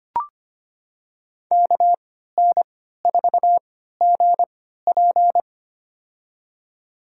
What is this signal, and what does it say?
Morse code, heard as a keyed tone, about 25 wpm, spaced as for 14 wpm, 700 Hz. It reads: KN4GP
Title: USA Call Signs - 5 Characters - 3x Character Spacing - 25wpm